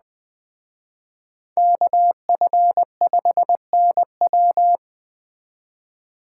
Morse code, heard as a keyed tone, about 20 wpm, 700 Hz. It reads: KF5NW